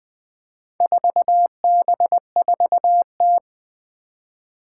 Morse code, heard as a keyed tone, about 20 wpm, 700 Hz. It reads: 4B4T
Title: Four Challenging Letter-Number Combinations 20wpm